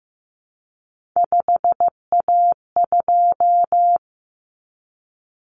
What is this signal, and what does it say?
Morse code, heard as a keyed tone, about 15 wpm, 700 Hz. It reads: 5A2
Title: Common Cut Numbers - 3-Digit 15wpm